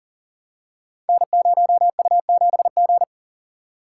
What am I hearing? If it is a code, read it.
Morse code, heard as a keyed tone, about 39 wpm, 690 Hz. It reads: N0U7G
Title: Code Groups - 5 Letter:Numbers - 40wpm